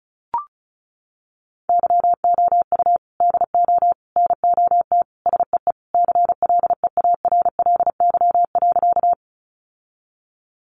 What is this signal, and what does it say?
Morse code, heard as a keyed tone, about 34 wpm, 700 Hz. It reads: YOU DO NOT SEE CLEARLY.
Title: Sentences from Top 1300 Words 35wpm